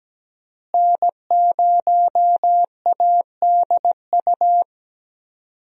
Morse code, heard as a keyed tone, about 17 wpm, 700 Hz. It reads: N0ADU
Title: USA Call Signs - 5 Characters - Repeated 2x - 17wpm